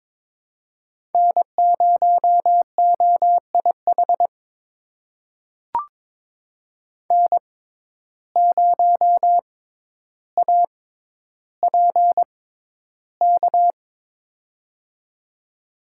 Morse code, heard as a keyed tone, about 22 wpm, spaced as for 11 wpm, 700 Hz. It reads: N0OIH N0APK
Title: USA Call Signs - 5 Characters - 6x Character Spacing - 22wpm